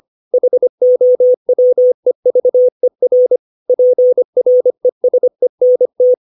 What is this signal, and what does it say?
Morse code, heard as a keyed tone, about 25 wpm, 500 Hz. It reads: HOWEVER PRESENT
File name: 632 words / 632 Words 25WPM